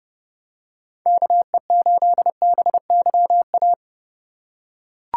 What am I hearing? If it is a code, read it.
Morse code, heard as a keyed tone, about 31 wpm, 700 Hz. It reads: KE8BYA
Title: USA Call Signs - 6 Characters - 2x Character Spacing - 30wpm